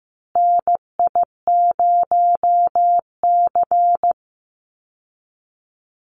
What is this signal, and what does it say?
Morse code, heard as a keyed tone, about 15 wpm, 700 Hz. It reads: NI0C
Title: Call Signs Repeated 2x - 15wpm